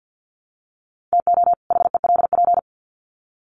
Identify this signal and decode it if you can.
Morse code, heard as a keyed tone, about 50 wpm, 700 Hz. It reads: TO HELP
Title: Sets of 2 Words 50wpm